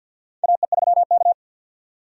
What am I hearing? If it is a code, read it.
Morse code, heard as a keyed tone, about 52 wpm, 700 Hz. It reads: AE3K
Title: USA Call Signs - 4 Characters - 50wpm